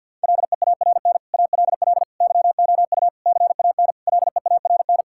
Morse code, heard as a keyed tone, about 50 wpm, 700 Hz. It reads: LEARN ALL YOU CAN LEARN
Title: Sentences from Top 1000 Words - Review Part 5 - 50wpm